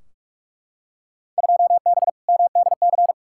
Morse code, heard as a keyed tone, about 43 wpm, 700 Hz. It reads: 2B MDC
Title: Field Day Exchanges - 45wpm